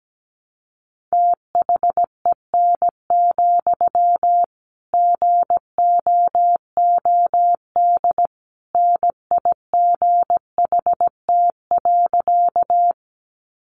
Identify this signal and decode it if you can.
Morse code, heard as a keyed tone, about 17 wpm, 700 Hz. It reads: THEN, GOOD NIGHT.
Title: Sentences from Top 300 Words 17wpm